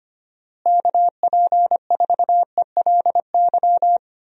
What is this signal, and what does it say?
Morse code, heard as a keyed tone, about 25 wpm, 700 Hz. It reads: KP4ELY
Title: USA Call Signs - 6 Characters - Repeated 3x - 25wpm